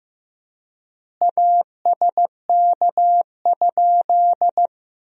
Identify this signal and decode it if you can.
Morse code, heard as a keyed tone, about 15 wpm, 700 Hz. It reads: ASK?
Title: 4x Word-Spacing - Sentences from Top 200 Words - 15wpm